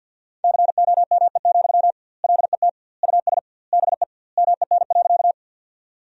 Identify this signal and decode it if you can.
Morse code, heard as a keyed tone, about 50 wpm, 700 Hz. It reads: COME, LET US BE MEN.
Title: 2x Word-Spacing - Sentences from Top 300 Words - 50wpm